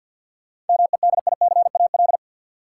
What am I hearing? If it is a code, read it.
Morse code, heard as a keyed tone, about 51 wpm, 700 Hz. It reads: MEDICAL
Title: Sentences from Top 800 Words - New Word Drill - 50wpm